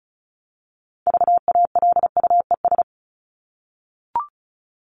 Morse code, heard as a keyed tone, about 34 wpm, 700 Hz. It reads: VALUES
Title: ES:S Suffix - Top 100 Words with Base Words 35wpm